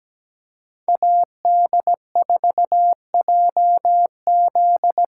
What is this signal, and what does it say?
Morse code, heard as a keyed tone, about 17 wpm, 700 Hz. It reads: AD4JZ
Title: USA Callsign DE Callsign - 5 Characters - 17wpm